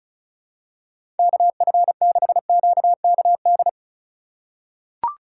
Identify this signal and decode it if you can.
Morse code, heard as a keyed tone, about 36 wpm, 700 Hz. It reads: KF6QKD
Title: USA Call Signs - 6 Characters - 4x Character Spacing - 35wpm